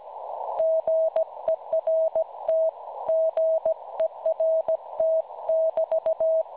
Morse code, heard as a keyed tone, about 17 wpm, 640 Hz. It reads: GERT GERT <BT>
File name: cwronde / websdr recording .6kHz